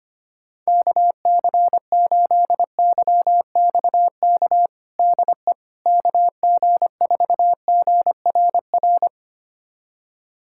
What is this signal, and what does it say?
Morse code, heard as a keyed tone, about 25 wpm, 700 Hz. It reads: KC8YXK DE KG4GRR
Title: USA Callsign DE Callsign - 6 Characters - 25wpm